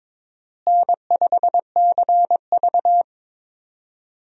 Morse code, heard as a keyed tone, about 22 wpm, 700 Hz. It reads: N5CV
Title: USA Call Signs - 4 Characters - Repeated 3x - 22wpm